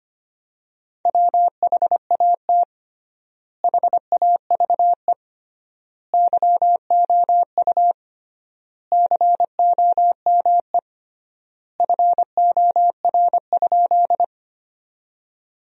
Morse code, heard as a keyed tone, about 25 wpm, 700 Hz. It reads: WHAT HAVE YOU COME FOR?
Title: 3x Word-Spacing - Sentences from Top 100 Words - 25wpm